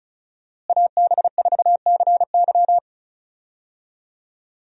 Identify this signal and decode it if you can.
Morse code, heard as a keyed tone, about 36 wpm, 700 Hz. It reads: AB4CY
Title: USA Call Signs - 5 Characters - Repeated 2x - 35wpm